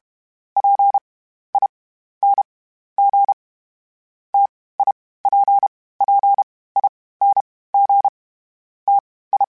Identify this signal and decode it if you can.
Morse code, heard as a keyed tone, about 32 wpm, spaced as for 16 wpm, 800 Hz. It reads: PING TIPPING TI
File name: Difficult Combos / Difficult Combos-mp3s / T AOEI @30